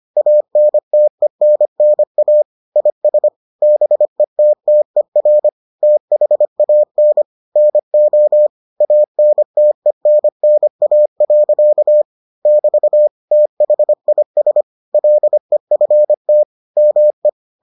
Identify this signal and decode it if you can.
Morse code, heard as a keyed tone, about 25 wpm, 600 Hz. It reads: ANTENNA IS BETTER THAN NO ANTENNA. <BT> THIS LEFT ME